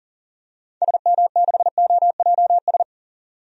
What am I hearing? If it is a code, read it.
Morse code, heard as a keyed tone, about 40 wpm, 700 Hz. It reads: SM6OJS
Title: DXCC-100 Call Signs - 6 Characters - 40wpm